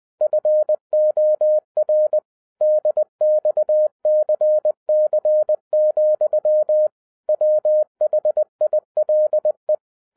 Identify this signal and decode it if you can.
Morse code, heard as a keyed tone, about 20 wpm, 610 Hz. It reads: FOR DXCC, WHILE
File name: 2012-01-mp3 / n05-CAME BACK TO HAM RADIO AFTER 30 YEARS OF INTERVAL-20wpm